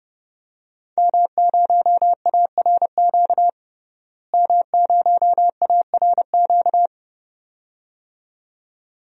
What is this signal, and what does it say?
Morse code, heard as a keyed tone, about 32 wpm, 700 Hz. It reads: M0ARQ M0ARQ
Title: DXCC-100 Call Signs - 5 Characters - Repeated 2x - 30wpm